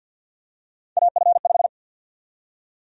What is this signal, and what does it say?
Morse code, heard as a keyed tone, about 53 wpm, 700 Hz. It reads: AF5
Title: USA Call Sign Prefixes - 50wpm